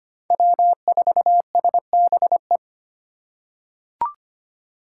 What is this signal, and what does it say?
Morse code, heard as a keyed tone, about 25 wpm, 700 Hz. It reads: W4SBE
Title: USA Call Signs - 5 Characters - 4x Character Spacing - 25wpm